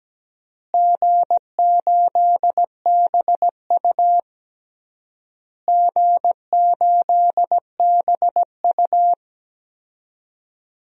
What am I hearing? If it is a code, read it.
Morse code, heard as a keyed tone, about 17 wpm, 700 Hz. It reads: G8BU G8BU
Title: DXCC-100 Call Signs - 4 Characters - Repeated 3x - 17wpm